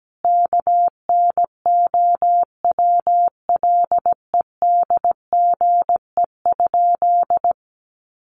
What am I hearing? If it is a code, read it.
Morse code, heard as a keyed tone, about 17 wpm, 700 Hz. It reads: KNOWLEDGE?
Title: Sentences from Top 600 Words 17wpm